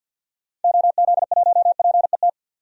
Morse code, heard as a keyed tone, about 48 wpm, 700 Hz. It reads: OZ1PET
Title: DXCC-100 Call Signs - 6 Characters - 50wpm